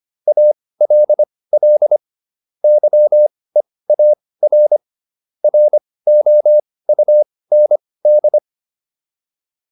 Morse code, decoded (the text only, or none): ALL YEAR ROUND